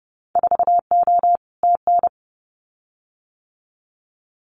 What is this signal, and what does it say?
Morse code, heard as a keyed tone, about 30 wpm, 700 Hz. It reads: 4O TN
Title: Winter Field Day Exchanges Encore 30wpm